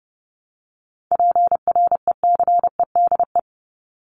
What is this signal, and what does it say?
Morse code, heard as a keyed tone, about 30 wpm, 700 Hz. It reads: PRECEDE
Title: ING Suffix - Top 100 Words with Base Words 30wpm